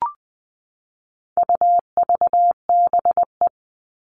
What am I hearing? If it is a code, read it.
Morse code, heard as a keyed tone, about 20 wpm, 700 Hz. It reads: UVBE
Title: All Cut Numbers - 4-Digit 20wpm